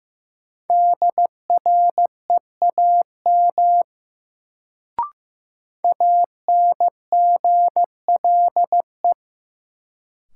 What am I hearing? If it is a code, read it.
Morse code, heard as a keyed tone, about 15 wpm, 700 Hz. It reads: DREAM ANGLE
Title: Sentences from Top 1500 Words - New Word Drill Alt - 15wpm